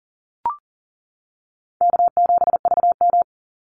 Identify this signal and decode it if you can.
Morse code, heard as a keyed tone, about 38 wpm, 700 Hz. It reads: K7VM
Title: Easy Call Signs - 40wpm